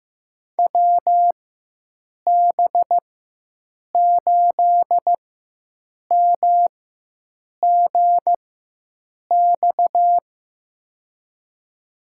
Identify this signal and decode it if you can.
Morse code, heard as a keyed tone, about 15 wpm, spaced as for 7 wpm, 700 Hz. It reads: WB8MGX
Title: USA Call Signs - 4x Character Spacing - 15wpm